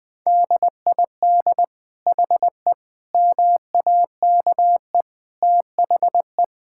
Morse code, heard as a keyed tone, about 19 wpm, 700 Hz. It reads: DID HE MAKE THE
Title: Sentences from Top 1600 Words 20wpm